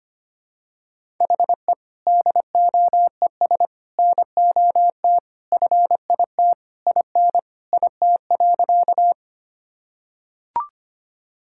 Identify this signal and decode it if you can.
Morse code, heard as a keyed tone, about 25 wpm, 700 Hz. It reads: HE DOES NOT FIT IN IT.
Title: Sentences from Top 1000 Words - Review Part 5 - 25wpm